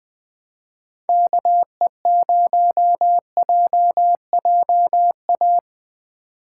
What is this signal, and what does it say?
Morse code, heard as a keyed tone, about 20 wpm, 700 Hz. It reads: KE0JJA
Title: USA Call Signs - 2x Character Spacing - 20wpm